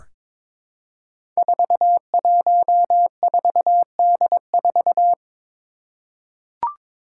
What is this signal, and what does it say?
Morse code, heard as a keyed tone, about 22 wpm, 700 Hz. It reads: 414D4
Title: Five Challenging Letter-Number Combinations 22wpm